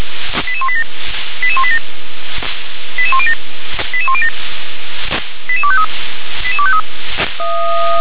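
Five-tone selective call (ZVEI), heard as quick runs of five tones, tones of about 70 ms. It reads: 88187 88187 88107 88107 88253 88253
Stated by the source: ZVEI five-tone selective call received on SDR.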